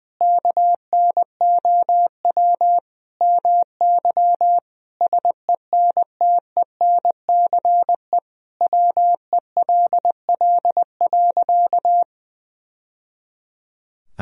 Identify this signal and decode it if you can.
Morse code, heard as a keyed tone, about 20 wpm, 700 Hz. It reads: KNOW MY SENTENCE WELL.
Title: Sentences from Top 1000 Words - Review Part 1 - 20wpm